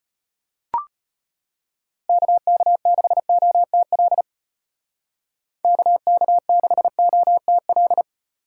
Morse code, heard as a keyed tone, about 34 wpm, 700 Hz. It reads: KK6OTL KK6OTL